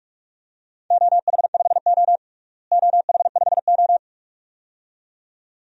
Morse code, heard as a keyed tone, about 46 wpm, 700 Hz. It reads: OH5O OH5O
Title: DXCC-100 Call Signs - 4 Characters - Repeated 2x - 45wpm